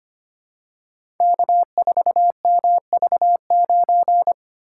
Morse code, heard as a keyed tone, about 25 wpm, 690 Hz. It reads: K4MV9
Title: Code Groups - 5 Letter:Numbers - 25wpm